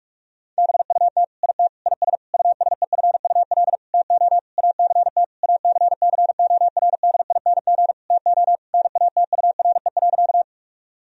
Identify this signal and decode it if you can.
Morse code, heard as a keyed tone, about 46 wpm, 700 Hz. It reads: BUT IT IS USEFUL TO ACT ACCORDING TO NATURE.
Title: Sentences from Top 1000 Words - Review Part 4 - 45wpm